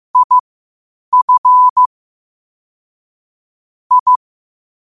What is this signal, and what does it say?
Morse code, heard as a keyed tone, about 15 wpm, spaced as for 8 wpm, 1000 Hz. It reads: IF I
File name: 8wpm